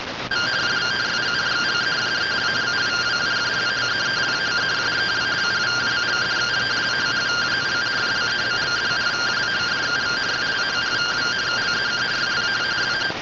Two-tone teleprinter signal, about 170 Hz shift.
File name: findings / CORONAL fr69 nb202 3mm.gen-43 rtty rx